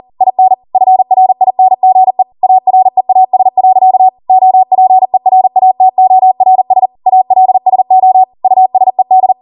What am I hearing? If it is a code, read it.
Morse code, heard as a keyed tone, about 40 wpm, 750 Hz. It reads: IN FRINGE AREAS. OPERATORS ALSO USED